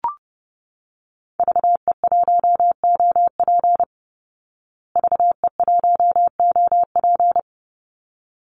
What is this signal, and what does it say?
Morse code, heard as a keyed tone, about 30 wpm, 700 Hz. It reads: VE1OP VE1OP
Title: Call Signs Repeated 3x - 30wpm